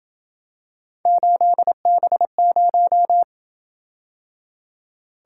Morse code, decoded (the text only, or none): 8B0